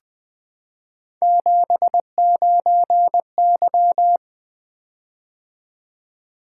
Morse code, heard as a keyed tone, about 20 wpm, 700 Hz. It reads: 79Y
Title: Three Letter-Number Combinations - 20wpm